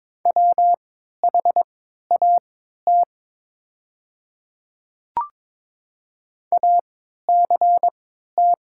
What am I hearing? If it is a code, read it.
Morse code, heard as a keyed tone, about 22 wpm, spaced as for 12 wpm, 700 Hz. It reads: WHAT ACT
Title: 3x Character Spacing - Top 200 Words - 22wpm